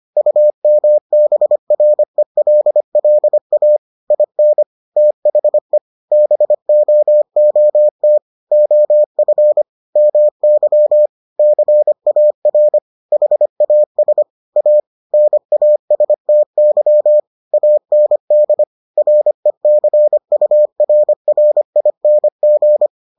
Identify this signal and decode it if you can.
Morse code, heard as a keyed tone, about 25 wpm, 600 Hz. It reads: UMBRELLA IN THE BOOT OF MY CAR HAS A NASTY AND RECURRING